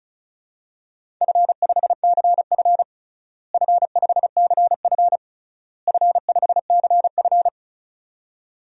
Morse code, heard as a keyed tone, about 36 wpm, 700 Hz. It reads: F5CF F5CF F5CF
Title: DXCC-100 Call Signs - 4 Characters - Repeated 3x - 35wpm